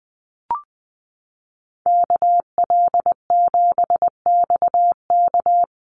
Morse code, heard as a keyed tone, about 20 wpm, 700 Hz. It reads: KL7XK
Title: USA Call Signs Repeated 2x 20wpm